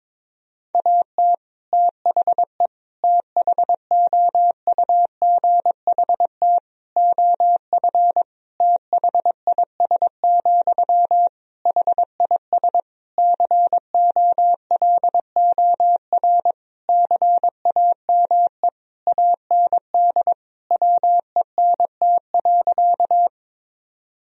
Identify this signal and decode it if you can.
Morse code, heard as a keyed tone, about 22 wpm, 700 Hz. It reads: AT THE THOUGHT OF THIS, HIS COLOR CAME AND WENT.
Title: Sentences from Top 1000 Words - Review Part 2 - 22wpm